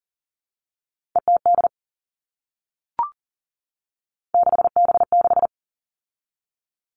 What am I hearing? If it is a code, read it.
Morse code, heard as a keyed tone, about 40 wpm, 700 Hz. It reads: ETD 6B6
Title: All Cut Numbers - 3-Digit 40wpm